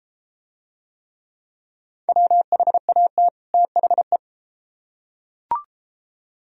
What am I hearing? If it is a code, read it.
Morse code, heard as a keyed tone, about 34 wpm, 700 Hz. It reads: WHAT THE